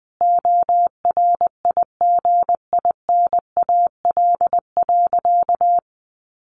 Morse code, heard as a keyed tone, about 20 wpm, 700 Hz. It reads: ORIGINAL.
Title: Sentences from Top 1400 Words 20wpm